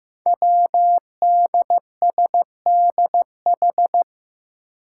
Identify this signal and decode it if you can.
Morse code, heard as a keyed tone, about 15 wpm, 700 Hz. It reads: WDSDH
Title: Five Challenging Letter-Number Combinations 15wpm